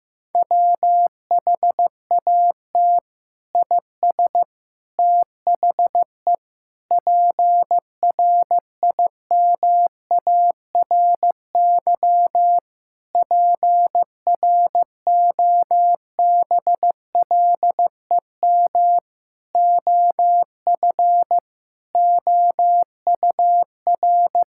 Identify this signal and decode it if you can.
Morse code, heard as a keyed tone, about 15 wpm, 700 Hz. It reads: WHAT IS THE PRIMARY PROBLEM OF OUR
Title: Sentences from Top 1000 Words - Review Part 5 - 15wpm